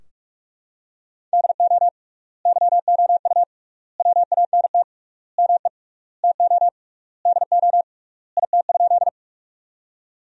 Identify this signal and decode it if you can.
Morse code, heard as a keyed tone, about 46 wpm, 700 Hz. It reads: DO YOU WANT ME TO DO IT?
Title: 3x Word-Spacing - Sentences from Top 200 Words - 45wpm